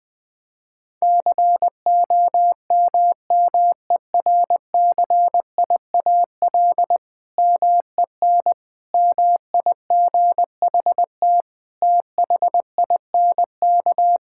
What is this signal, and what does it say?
Morse code, heard as a keyed tone, about 21 wpm, 700 Hz. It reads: COMMERCIAL MEN MIGHT THINK
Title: Sentences from Top 1000 Words - Review Part 5 - 20wpm